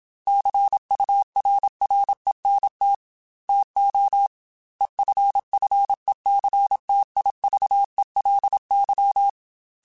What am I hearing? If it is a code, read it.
Morse code, heard as a keyed tone, about 27 wpm, 800 Hz. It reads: CURRENT TO EFFECTIVELY
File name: Technical Subjects / Technical Subjects-mp3s / XC @25